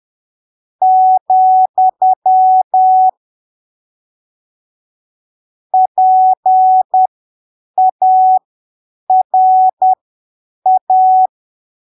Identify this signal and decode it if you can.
Morse code, heard as a keyed tone, about 10 wpm, spaced as for 7 wpm, 750 Hz. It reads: , PARA
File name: arquivos / texto / texto 09 10ppm